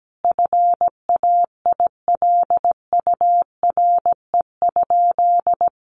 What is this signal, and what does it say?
Morse code, heard as a keyed tone, about 17 wpm, 700 Hz. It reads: FAILURE?
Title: Sentences from Top 1200 Words 17wpm